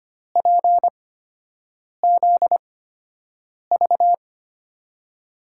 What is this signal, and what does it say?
Morse code, heard as a keyed tone, about 25 wpm, spaced as for 7 wpm, 700 Hz. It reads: PZV
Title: USA Call Signs - 5 Characters - 8x Character Spacing - 25wpm